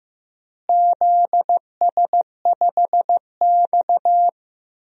Morse code, heard as a keyed tone, about 15 wpm, 700 Hz. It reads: ZS5X
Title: DXCC-100 Call Signs - 4 Characters - Repeated 3x - 15wpm